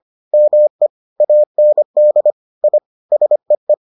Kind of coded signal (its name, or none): Morse code